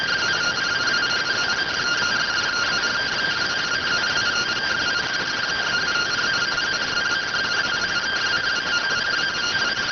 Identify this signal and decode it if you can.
Two-tone teleprinter signal, about 170 Hz shift.